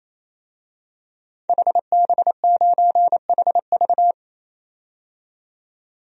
Morse code, heard as a keyed tone, about 28 wpm, 700 Hz. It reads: HB9HV
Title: DXCC-100 Call Signs - 5 Characters - 28wpm